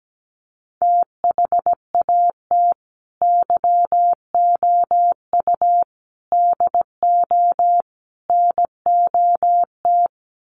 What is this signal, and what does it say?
Morse code, heard as a keyed tone, about 17 wpm, 700 Hz. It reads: THAT YOU DO NOT
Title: Sets of 4 Words 17wpm